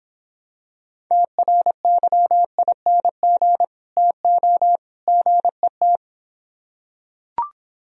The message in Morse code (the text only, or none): TRYING TO GET